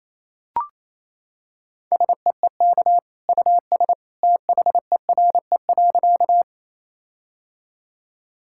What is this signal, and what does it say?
Morse code, heard as a keyed tone, about 28 wpm, 700 Hz. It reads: SEEK US THERE.
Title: Sentences from Top 1600 Words 28wpm